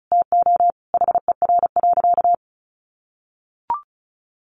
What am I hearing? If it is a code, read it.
Morse code, heard as a keyed tone, about 34 wpm, 700 Hz. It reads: TO HER.
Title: Sentences from Top 1500 Words 35wpm